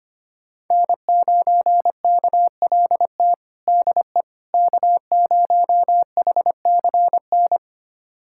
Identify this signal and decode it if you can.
Morse code, heard as a keyed tone, about 25 wpm, 700 Hz. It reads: N9KLT DE K0HCN
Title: USA Callsign DE Callsign - 5 Characters - 25wpm